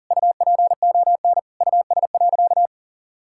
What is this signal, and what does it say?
Morse code, heard as a keyed tone, about 40 wpm, 700 Hz. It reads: UPON US.
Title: Sentences from Top 1600 Words 40wpm